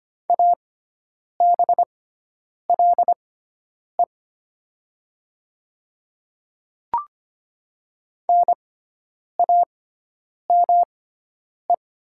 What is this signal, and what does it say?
Morse code, heard as a keyed tone, about 25 wpm, spaced as for 9 wpm, 700 Hz. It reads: ABLE NAME
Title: 6x Character Spacing - Top 500 Words - 25wpm